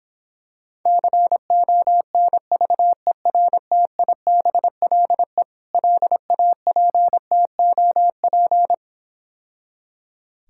Morse code, heard as a keyed tone, about 26 wpm, 700 Hz. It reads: CONVERTIBLE LAPTOP